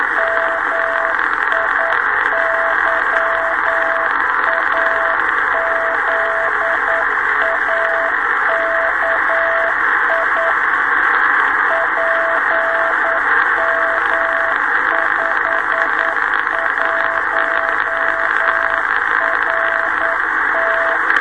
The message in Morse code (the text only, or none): MIYAZAKI PM51RT